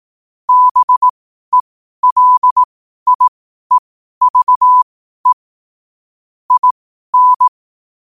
Morse code, heard as a keyed tone, about 18 wpm, spaced as for 12 wpm, 1000 Hz. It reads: BELIEVE IN
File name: qotdfiles / 12wpm